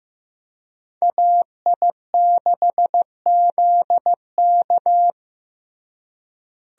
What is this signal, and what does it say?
Morse code, heard as a keyed tone, about 15 wpm, 700 Hz. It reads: AI6ZK
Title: USA Call Signs - 5 Characters - Repeated 3x - 15wpm